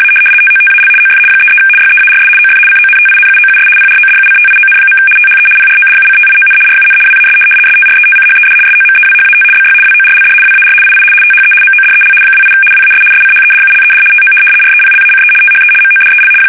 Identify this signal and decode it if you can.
RTTY, about 75 baud. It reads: A TOTAL OF 33 YEARS OF CONTINUOUS PRODUCTION. THE PRODUCTION RUN WAS STRETCHED SOMEWHAT BY WORLD WAR II THE MODEL 28 WAS SCHEDULED TO REPLACE THE MODEL
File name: fsk850-75B / teletype-corp